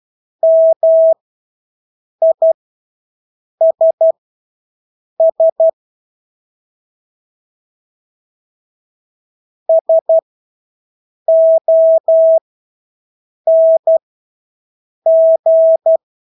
Morse code, heard as a keyed tone, about 12 wpm, spaced as for 6 wpm, 650 Hz. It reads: MISS SONG
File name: Engelske ord 12-6 wpm